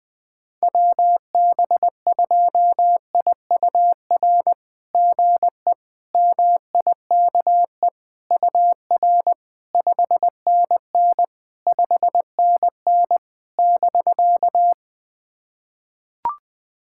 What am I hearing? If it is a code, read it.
Morse code, heard as a keyed tone, about 20 wpm, 690 Hz. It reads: WB2IUR GE MIKE UR 5NN 5NN <BK>
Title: SOTA QSO Practice 20wpm